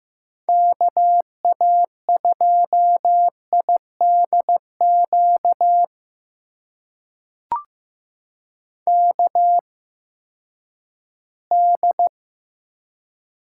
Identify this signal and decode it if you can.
Morse code, heard as a keyed tone, about 15 wpm, spaced as for 10 wpm, 700 Hz. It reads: KA2IDQ KD
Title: USA Call Signs - 6 Characters - 8x Character Spacing - 15wpm